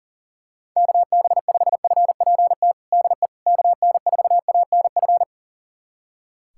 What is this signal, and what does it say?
Morse code, heard as a keyed tone, about 40 wpm, 700 Hz. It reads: KB5FPT DE KN4ANF
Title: USA Callsign DE Callsign - 6 Characters - 40wpm